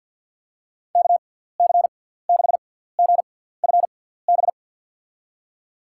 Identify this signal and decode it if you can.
Morse code, heard as a keyed tone, about 48 wpm, spaced as for 17 wpm, 700 Hz. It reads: KC6GFB
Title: USA Call Signs - 6 Characters - 6x Character Spacing - 50wpm